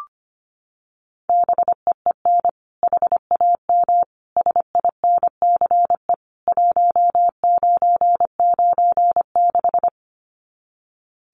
Morse code, heard as a keyed tone, about 25 wpm, 700 Hz. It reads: BEEN HAM SINCE 1996
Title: Common QSO Phrases 25wpm